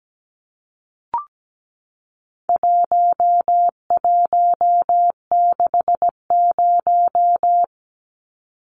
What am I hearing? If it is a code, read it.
Morse code, heard as a keyed tone, about 17 wpm, 700 Hz. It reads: 1160
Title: Numbers - 4-Digit 17wpm